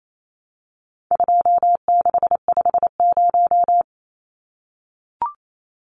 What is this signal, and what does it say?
Morse code, heard as a keyed tone, about 28 wpm, 700 Hz. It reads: 2650
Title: Numbers - 4-Digit 28wpm